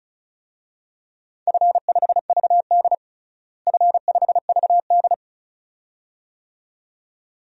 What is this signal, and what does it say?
Morse code, heard as a keyed tone, about 36 wpm, 690 Hz. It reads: F5VD F5VD
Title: DXCC-100 Call Signs - 4 Characters - Repeated 2x - 35wpm